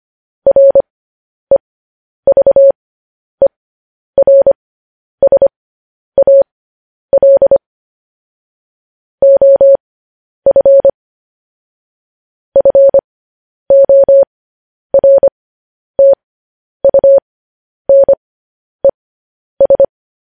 Morse code, heard as a keyed tone, about 25 wpm, spaced as for 10 wpm, 550 Hz. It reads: REVERSAL OF FORTUNES
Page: news headlines in morse code at 10 wpm